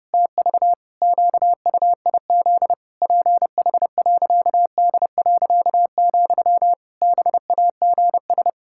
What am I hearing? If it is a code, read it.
Morse code, heard as a keyed tone, about 30 wpm, 700 Hz. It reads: TV QUIZ PH.D., BAGS
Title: Pangram Challenge v2 - 30wpm